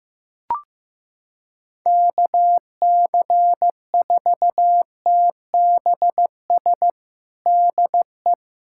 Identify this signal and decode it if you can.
Morse code, heard as a keyed tone, about 15 wpm, 700 Hz. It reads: KC4TBS DE
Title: USA Callsign DE Callsign - 6 Characters - 15wpm